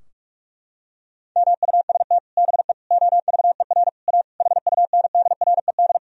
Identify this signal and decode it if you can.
Morse code, heard as a keyed tone, about 46 wpm, 700 Hz. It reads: MUST BE OVER A HUNDRED
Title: Sentences from Top 1000 Words - Review Part 2 - 45wpm